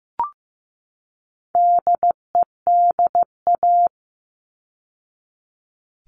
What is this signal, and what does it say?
Morse code, heard as a keyed tone, about 15 wpm, 700 Hz. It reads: DEDA
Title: All Cut Numbers - 4-Digit 15wpm